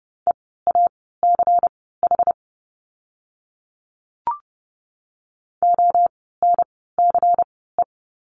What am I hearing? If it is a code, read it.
Morse code, heard as a keyed tone, about 30 wpm, spaced as for 17 wpm, 700 Hz. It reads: EACH ONCE
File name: Lesson 16f - Letter c - Words 30wpm